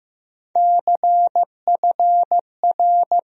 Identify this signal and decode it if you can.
Morse code, heard as a keyed tone, about 15 wpm, 700 Hz. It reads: CFR
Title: USA Call Sign Postfixes - 3 Letters 15wpm